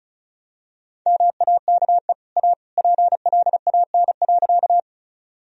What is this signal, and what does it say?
Morse code, heard as a keyed tone, about 36 wpm, 700 Hz. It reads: MAKE A PLAN.
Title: Sentences from Top 1000 Words - Review Part 2 - 35wpm